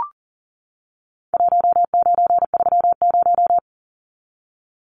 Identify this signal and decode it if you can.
Morse code, heard as a keyed tone, about 40 wpm, 700 Hz. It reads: 1930
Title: Numbers - Contest Style - 4-Digit 40wpm